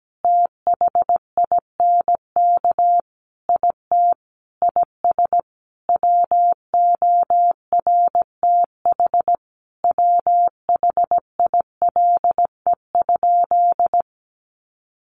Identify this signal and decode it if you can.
Morse code, heard as a keyed tone, about 17 wpm, 700 Hz. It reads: THINK IT IS WORTH WHILE?
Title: Sentences from Top 800 Words 17wpm